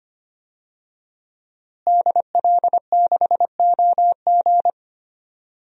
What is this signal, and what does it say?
Morse code, heard as a keyed tone, about 25 wpm, 700 Hz. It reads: DL6OG
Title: DXCC-100 Call Signs - 5 Characters - Repeated 2x - 25wpm